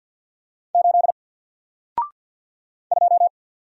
Morse code, heard as a keyed tone, about 48 wpm, 700 Hz. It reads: 8 2
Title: Single Letter-Number Alt 50wpm